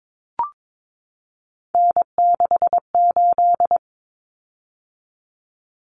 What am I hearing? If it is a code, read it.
Morse code, heard as a keyed tone, about 22 wpm, 700 Hz. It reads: N68
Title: Common Cut Numbers - 3-Digit 22wpm